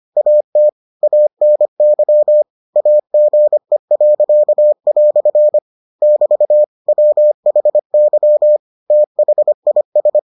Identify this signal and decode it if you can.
Morse code, heard as a keyed tone, about 25 wpm, 600 Hz. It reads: AT ANY AGE." <BT> WHY THIS